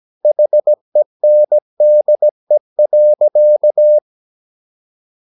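Morse code, heard as a keyed tone, about 17 wpm, 600 Hz. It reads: HENDE.
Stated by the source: Slow (QRS) Morse practice at 17 wpm, from CW med Gnister.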